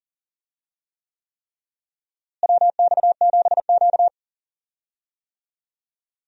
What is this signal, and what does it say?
Morse code, heard as a keyed tone, about 37 wpm, 700 Hz. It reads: WX7Q
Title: USA Call Signs - 4 Characters - Repeated 3x - 40wpm